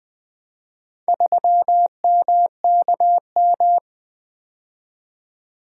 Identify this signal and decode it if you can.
Morse code, heard as a keyed tone, about 20 wpm, 700 Hz. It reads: 3MKM
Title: Four Letter-Number Combinations - 20wpm